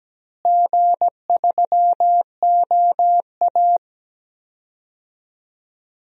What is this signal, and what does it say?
Morse code, heard as a keyed tone, about 17 wpm, 700 Hz. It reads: G3OA
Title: DXCC-100 Call Signs - 4 Characters - Repeated 2x - 17wpm